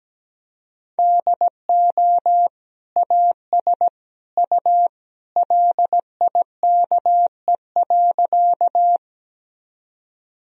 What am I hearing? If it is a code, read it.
Morse code, heard as a keyed tone, about 17 wpm, 700 Hz. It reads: DO AS U LIKE.
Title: Sentences from Top 100 Words and Abbreviations 17wpm